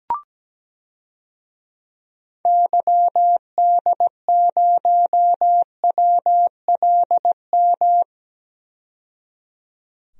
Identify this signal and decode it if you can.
Morse code, heard as a keyed tone, about 17 wpm, 700 Hz. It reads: YD0WLM
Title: DXCC-100 Call Signs - 6 Characters - 17wpm